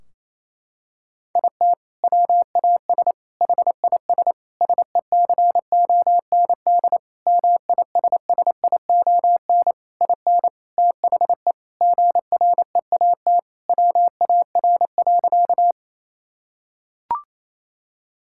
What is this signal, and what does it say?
Morse code, heard as a keyed tone, about 28 wpm, 700 Hz. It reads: IT WAS HIS SECOND MISSION IN THE GREAT WAR.
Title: Sentences from Top 1600 Words 28wpm